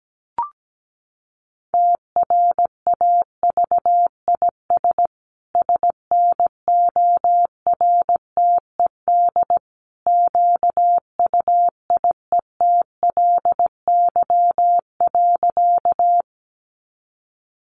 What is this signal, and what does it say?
Morse code, heard as a keyed tone, about 17 wpm, 700 Hz. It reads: TRAVIS SNORTED QUIETLY.
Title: A Sound Of Thunder 17wpm